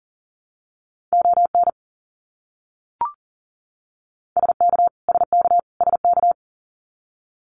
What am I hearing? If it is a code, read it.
Morse code, heard as a keyed tone, about 40 wpm, 700 Hz. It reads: ON SK SK SK
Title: Canadian Provinces and Territories - Repeated 3x - 40wpm